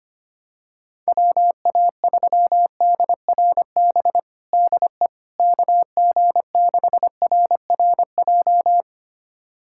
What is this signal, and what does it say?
Morse code, heard as a keyed tone, about 25 wpm, 700 Hz. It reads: WA3DRB DE KG6RRJ
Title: USA Callsign DE Callsign - 6 Characters - 25wpm